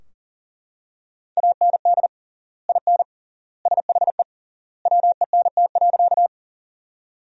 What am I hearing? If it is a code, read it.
Morse code, heard as a keyed tone, about 40 wpm, 700 Hz. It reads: AND IN SHE WENT.
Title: 3x Word-Spacing - Sentences from Top 200 Words - 40wpm